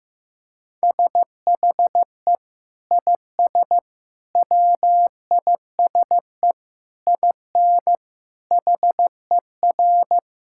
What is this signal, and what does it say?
Morse code, heard as a keyed tone, about 15 wpm, 700 Hz. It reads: SHE IS WISE IN HER
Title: Sentences from Top 1600 Words 15wpm